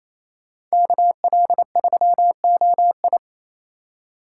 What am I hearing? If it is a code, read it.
Morse code, heard as a keyed tone, about 34 wpm, 700 Hz. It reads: KL3OI
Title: Speed Racing - USA 2x2 Call Signs 28wpm